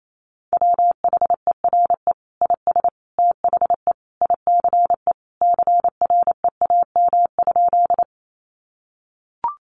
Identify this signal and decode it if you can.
Morse code, heard as a keyed tone, about 28 wpm, 700 Hz. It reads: WHERE IS THE ICE CREAM?
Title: Sentences from Top 1500 Words 28wpm